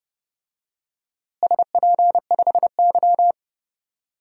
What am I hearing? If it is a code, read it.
Morse code, heard as a keyed tone, about 30 wpm, 700 Hz. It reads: SP5Y
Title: DXCC-100 Call Signs - 4 Characters - Repeated 2x - 30wpm